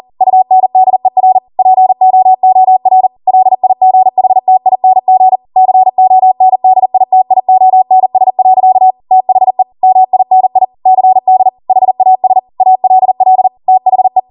Morse code, heard as a keyed tone, about 40 wpm, 750 Hz. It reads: UNDER POOR LIGHTING CONDITIONS. THE MINI CD HAS ALL THE